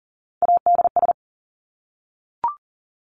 Morse code, heard as a keyed tone, about 40 wpm, 700 Hz. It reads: ADS
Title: Abbreviations Repeated 3x - 40wpm